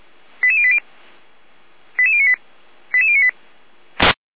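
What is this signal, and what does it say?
Five-tone selective call (ZVEI), heard as quick runs of five tones, tones of about 70 ms. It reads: 88080 88098 88098
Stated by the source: ZVEI five-tone selective call received on an SDR.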